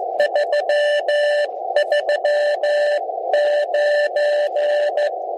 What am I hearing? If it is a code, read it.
Morse code, heard as a keyed tone, about 15 wpm, spaced as for 13 wpm, 590 Hz, with radio noise. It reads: 339
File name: ABI-SON-QSO-1903z- -7020r0